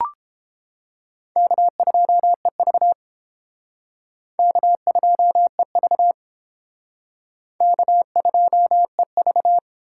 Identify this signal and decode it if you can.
Morse code, heard as a keyed tone, about 29 wpm, 700 Hz. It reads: K2EV K2EV K2EV